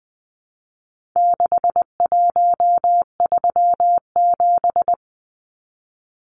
Morse code, heard as a keyed tone, about 20 wpm, 700 Hz. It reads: 6137
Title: Numbers - 4-Digit 20wpm